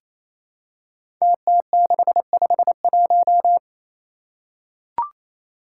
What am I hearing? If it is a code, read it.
Morse code, heard as a keyed tone, about 28 wpm, 700 Hz. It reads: TT651
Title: Five Challenging Letter-Number Combinations 28wpm